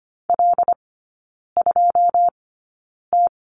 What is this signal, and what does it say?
Morse code, heard as a keyed tone, about 25 wpm, spaced as for 9 wpm, 700 Hz. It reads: L2T
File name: course / 25:10wpm / 093 Lesson 16e - Letter c - Three Characters 25wpm